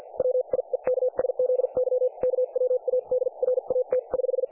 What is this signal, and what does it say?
Morse code, heard as a keyed tone, about 50 wpm, spaced as for 40 wpm, 500 Hz, with radio noise. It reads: WIEKSZY KWADRAT<HH>
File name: dlazaawansowanych / nagraniazpasma / websdr recording 3564.7kHz